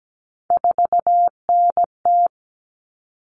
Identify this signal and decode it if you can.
Morse code, heard as a keyed tone, about 17 wpm, 700 Hz. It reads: 4NT
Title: Common Cut Numbers - 3-Digit 17wpm